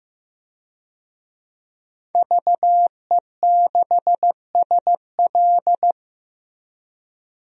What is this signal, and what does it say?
Morse code, heard as a keyed tone, about 15 wpm, 700 Hz. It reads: VE6SL
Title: DXCC-100 Call Signs - 5 Characters - Repeated 2x - 15wpm